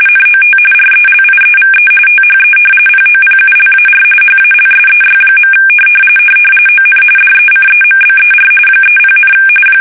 RTTY, about 45.45 baud. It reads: NOT RELIEVE THE POPULAR HALSEY. ON DECEMBER 30,
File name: fsk850 / battleshipnj